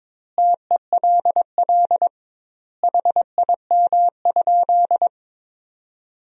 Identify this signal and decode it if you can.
Morse code, heard as a keyed tone, about 22 wpm, 700 Hz. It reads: TELL HIM?
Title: 2x Word-Spacing - Sentences from Top 200 Words - 22wpm